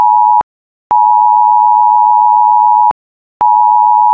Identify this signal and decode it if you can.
Emergency Alert System (SAME) tones (the 853 and 960 Hz attention tone). It is heard from 0.9 to 3.0 s.